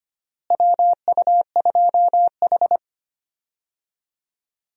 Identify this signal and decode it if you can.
Morse code, heard as a keyed tone, about 25 wpm, 700 Hz. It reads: WU2H
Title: USA Call Signs - 4 Characters - Repeated 3x - 25wpm